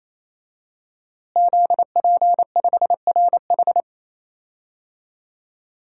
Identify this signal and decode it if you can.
Morse code, heard as a keyed tone, about 28 wpm, 700 Hz. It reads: ZP5RH
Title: DXCC-100 Call Signs - 5 Characters - Repeated 3x - 28wpm